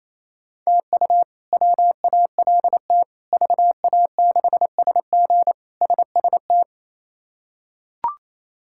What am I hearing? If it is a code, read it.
Morse code, heard as a keyed tone, about 28 wpm, 700 Hz. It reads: TU WALT VA6SG SST
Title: K1USN SST QSO Practice - 28wpm